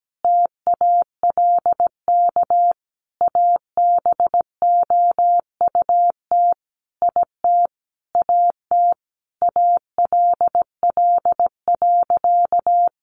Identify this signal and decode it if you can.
Morse code, heard as a keyed tone, about 17 wpm, 700 Hz. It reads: TALK ABOUT IT AT ALL.
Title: Sentences from Top 800 Words 17wpm